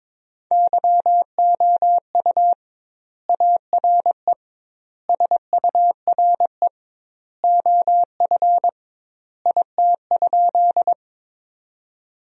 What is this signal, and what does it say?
Morse code, heard as a keyed tone, about 22 wpm, 700 Hz. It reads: YOU ARE SURE OF IT?
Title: 2x Word-Spacing - Sentences from Top 300 Words - 22wpm